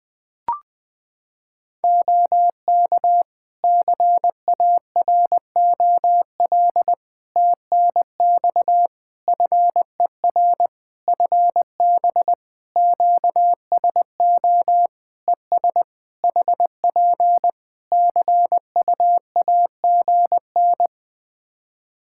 Morse code, heard as a keyed tone, about 20 wpm, 700 Hz. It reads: OK CAROL TNX FER FB QSO ES HP CUAGN
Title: Practice QSO - Exchange 4 - Ending - 20wpm